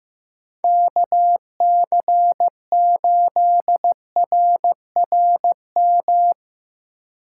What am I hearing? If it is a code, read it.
Morse code, heard as a keyed tone, about 15 wpm, 700 Hz. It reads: KC8RRM
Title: USA Call Signs - 8x Character Spacing - 15wpm